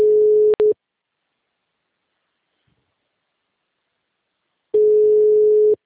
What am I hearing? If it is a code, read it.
Ringback tone (Europe, 425 Hz).